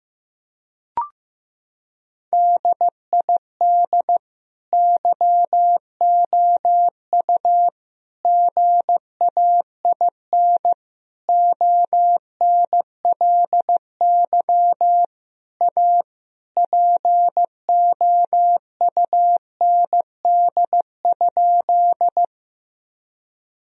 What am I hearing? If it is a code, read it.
Morse code, heard as a keyed tone, about 15 wpm, 700 Hz. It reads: DID YOU GAIN ONLY A POUND?
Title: Sentences from Top 1000 Words - Review Part 2 - 15wpm